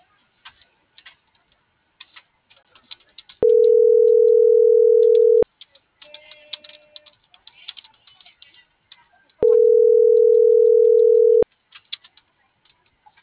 Ringback tone (North America), 440 + 480 Hz.